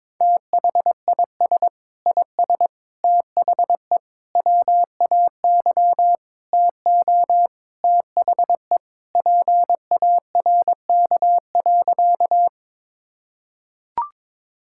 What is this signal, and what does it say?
Morse code, heard as a keyed tone, about 21 wpm, 700 Hz. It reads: THIS IS THE WAY TO THE PARK.
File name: practice / 22wpm / Sentences with 4-Word Phrases - 22wpm